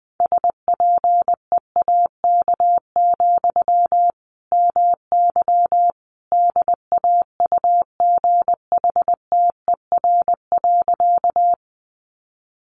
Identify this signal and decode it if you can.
Morse code, heard as a keyed tone, about 20 wpm, 700 Hz. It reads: SPEAK, MY DAUGHTER.
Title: Sentences from Top 1500 Words 20wpm